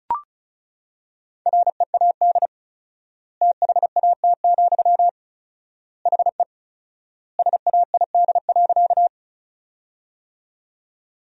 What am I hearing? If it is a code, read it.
Morse code, heard as a keyed tone, about 34 wpm, 700 Hz. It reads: READ THAT, HE SAID.
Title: 4x Word-Spacing - Sentences from Top 200 Words - 35wpm